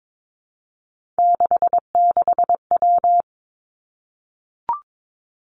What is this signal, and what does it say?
Morse code, heard as a keyed tone, about 22 wpm, 700 Hz. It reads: 66W
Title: Three Challenging Letter-Number Combinations 22wpm